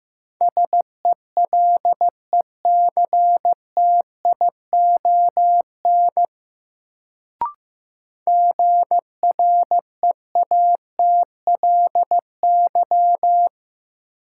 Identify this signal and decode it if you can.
Morse code, heard as a keyed tone, about 15 wpm, 700 Hz. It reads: SELECTION GREATLY
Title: Sentences from Top 1400 Words - New Word Drill - 15wpm